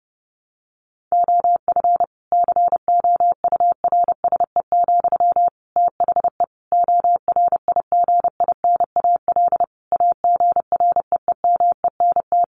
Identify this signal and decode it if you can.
Morse code, heard as a keyed tone, about 30 wpm, 700 Hz. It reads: OF COURSE, THE ORIGINAL AGREEMENT
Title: Sentences from Top 1500 Words 30wpm